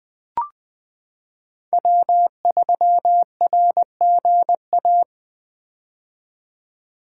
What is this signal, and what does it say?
Morse code, heard as a keyed tone, about 21 wpm, 700 Hz. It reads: W3RGA
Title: K1USN SST QSO Practice - 20wpm